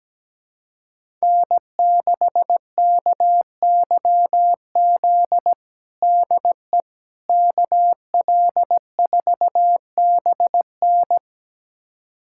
Morse code, heard as a keyed tone, about 17 wpm, 700 Hz. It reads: N6KYZ DE KL4BN
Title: USA Callsign DE Callsign - 5 Characters - 17wpm